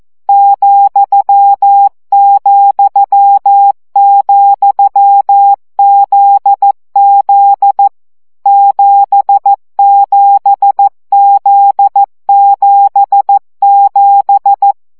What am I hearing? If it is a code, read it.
Morse code, heard as a keyed tone, about 14 wpm, 800 Hz. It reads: ,,,ZZ 77Z77
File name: CW gyakorlas z 7 vesszo 12 baud